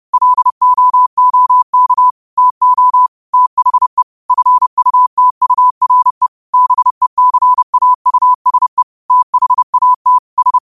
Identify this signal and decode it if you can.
Morse code, heard as a keyed tone, about 30 wpm, 1000 Hz. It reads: LOOK TO THE FUTURE BECAUSE THAT S
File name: qotdfiles / 30wpm